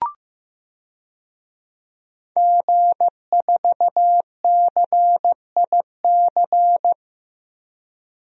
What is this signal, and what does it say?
Morse code, heard as a keyed tone, about 15 wpm, 700 Hz. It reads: G4CIC
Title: DXCC-100 Call Signs - 5 Characters - Repeated 3x - 15wpm